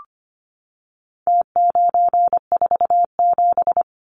Morse code, heard as a keyed tone, about 25 wpm, 700 Hz. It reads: T947
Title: Numbers - Contest Style - 4-Digit 25wpm